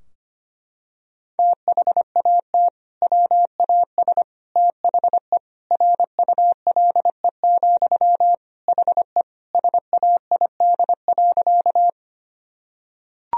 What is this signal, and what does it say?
Morse code, heard as a keyed tone, about 25 wpm, 700 Hz. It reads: THAT WAS THE RULE, HE SAID.
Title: Sentences from Top 1000 Words - Review Part 2 - 25wpm